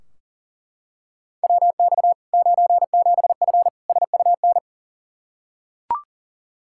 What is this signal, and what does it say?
Morse code, heard as a keyed tone, about 40 wpm, 700 Hz. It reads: WX 97F SUN
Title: Practice QSO - Exchange 2 - Info - 40wpm